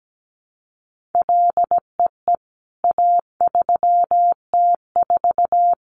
Morse code, heard as a keyed tone, about 17 wpm, 700 Hz. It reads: LEE A3T4
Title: CWT Exchange 17wpm